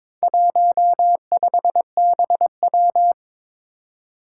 Morse code, heard as a keyed tone, about 21 wpm, 700 Hz. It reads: 15BW
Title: Four Challenging Letter-Number Combinations 22wpm